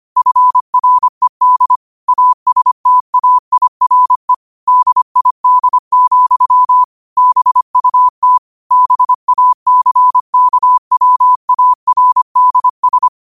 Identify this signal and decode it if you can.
Morse code, heard as a keyed tone, about 25 wpm, 1000 Hz. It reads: FRED ASTAIRE DID, BUT BACKWARDS
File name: qotdfiles / 25wpm